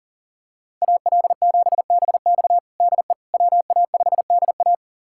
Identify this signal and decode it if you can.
Morse code, heard as a keyed tone, about 40 wpm, 700 Hz. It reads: AL7BX DE WA5DA
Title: USA Callsign DE Callsign - 5 Characters - 40wpm